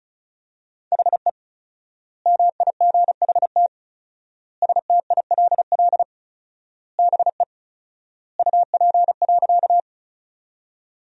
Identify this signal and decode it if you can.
Morse code, heard as a keyed tone, about 36 wpm, 700 Hz. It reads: HE MIGHT STILL BE UP.
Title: 4x Word-Spacing - Sentences from Top 300 Words - 35wpm